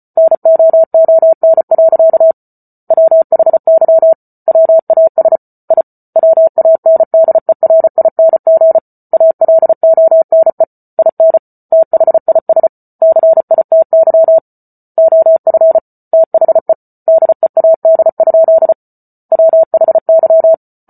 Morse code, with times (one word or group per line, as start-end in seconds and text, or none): 0.2-2.3 NOON.
2.9-4.1 WHY
4.5-5.4 WAS
5.7-5.8 I
6.2-8.8 WANDERING
9.1-10.6 ALONE
11.0-11.4 IN
11.7-12.7 THIS
13.0-14.4 CITY
15.0-15.8 OF
16.1-16.7 THE
17.1-18.7 DEAD?
19.3-20.5 WHY